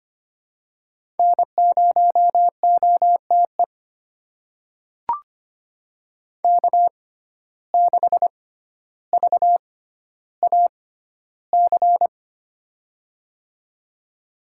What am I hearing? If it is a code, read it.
Morse code, heard as a keyed tone, about 25 wpm, spaced as for 13 wpm, 700 Hz. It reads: N0OTE K6VAC
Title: USA Call Signs - 5 Characters - 6x Character Spacing - 25wpm